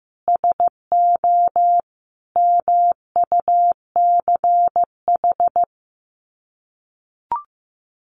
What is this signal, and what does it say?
Morse code, heard as a keyed tone, about 15 wpm, 700 Hz. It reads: SO MUCH
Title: Sets of 4 Words 15wpm